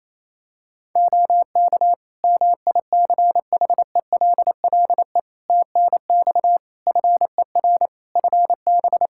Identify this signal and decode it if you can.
Morse code, heard as a keyed tone, about 28 wpm, 700 Hz. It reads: OK MICHELLE TNX FER FB
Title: Practice QSO - Exchange 4 - Ending - 28wpm